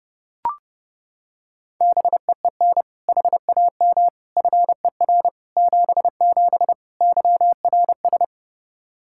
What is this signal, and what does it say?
Morse code, heard as a keyed tone, about 30 wpm, 700 Hz. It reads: BEEN HAM FER 77 YRS
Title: Practice QSO - Exchange 3 - More Info - 30wpm